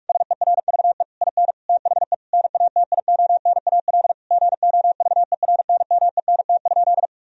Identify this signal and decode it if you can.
Morse code, heard as a keyed tone, about 47 wpm, 700 Hz. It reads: SERVE IN THE NATIONAL GOVERNMENT?
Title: Sentences from Top 1000 Words - Review Part 4 - 45wpm